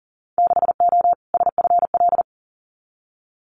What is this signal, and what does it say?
Morse code, heard as a keyed tone, about 42 wpm, 700 Hz. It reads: 6O SFL
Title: Winter Field Day Exchanges Encore 40wpm